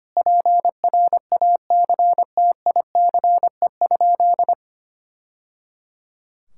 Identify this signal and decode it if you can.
Morse code, heard as a keyed tone, about 25 wpm, 700 Hz. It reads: PRACTICE?
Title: Sentences from Top 1000 Words - Review Part 5 - 25wpm